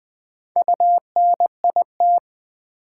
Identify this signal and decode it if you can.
Morse code, heard as a keyed tone, about 20 wpm, 700 Hz. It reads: UNIT
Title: Sentences from Top 1000 Words - Review Part 2 - New Word Drill - 20wpm